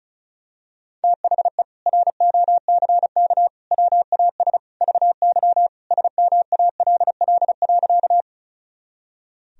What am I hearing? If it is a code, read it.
Morse code, heard as a keyed tone, about 36 wpm, 700 Hz. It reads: THE ROCK WAS VY SMALL.
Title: Sentences from Top 400 Words and Abbreviations 35wpm